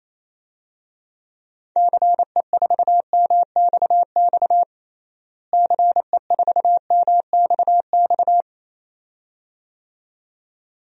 Morse code, heard as a keyed tone, about 28 wpm, 700 Hz. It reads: CE4MXX CE4MXX
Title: DXCC-100 Call Signs - 6 Characters - Repeated 2x - 28wpm